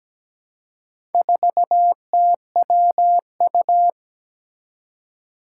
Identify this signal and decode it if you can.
Morse code, heard as a keyed tone, about 17 wpm, 700 Hz. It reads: 4TWU
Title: Four Challenging Letter-Number Combinations 17wpm